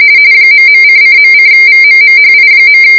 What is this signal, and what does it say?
RTTY, about 45.45 baud. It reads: ..::.:VMMHI.